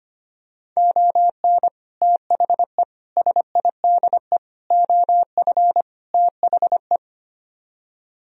Morse code, heard as a keyed tone, about 25 wpm, 700 Hz. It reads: ON THE SIDE OF THE
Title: Speed-Racing - Sets of 5 Words 25wpm